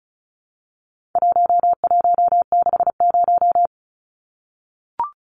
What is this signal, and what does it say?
Morse code, heard as a keyed tone, about 34 wpm, 700 Hz. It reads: 1160
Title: Numbers - 4-Digit 35wpm